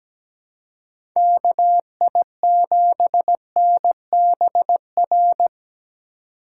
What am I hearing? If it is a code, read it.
Morse code, heard as a keyed tone, about 17 wpm, 700 Hz. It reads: KI7NBR
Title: USA Call Signs - 6 Characters - 2x Character Spacing - 17wpm